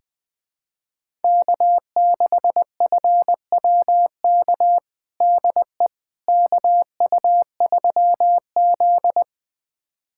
Morse code, heard as a keyed tone, about 20 wpm, 700 Hz. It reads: K6FWK DE KU3Z
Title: USA Callsign DE Callsign 20wpm